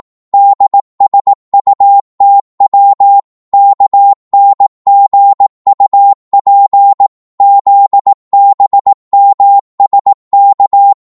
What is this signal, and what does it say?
Morse code, heard as a keyed tone, about 18 wpm, 800 Hz. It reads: DSUTW KNGUP ZBMSK
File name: Morse / ltr-18wpm